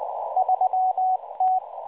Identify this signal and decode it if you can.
Morse code, heard as a keyed tone, about 18 wpm, 750 Hz, with radio noise. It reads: 3T
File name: HM02 5-14-15 0140 AM UTC 7351kHz CW